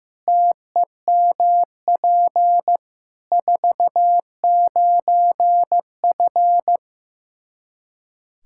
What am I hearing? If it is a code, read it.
Morse code, heard as a keyed tone, about 15 wpm, 700 Hz. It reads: TEMP 49F
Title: K1USN SST QSO Practice 15wpm